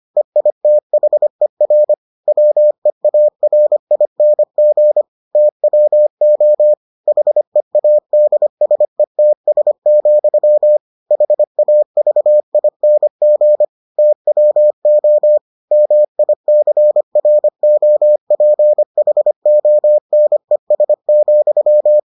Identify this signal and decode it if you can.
Morse code, heard as a keyed tone, about 25 wpm, 600 Hz. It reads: EITHER WEARING TWO HEADSETS, HAVING TWO MICROPHONES,